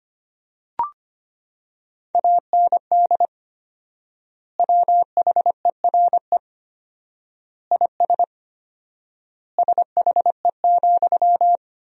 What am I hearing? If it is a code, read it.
Morse code, heard as a keyed tone, about 25 wpm, 700 Hz. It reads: AND WHERE IS SHE,
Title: 4x Word-Spacing - Sentences from Top 200 Words - 25wpm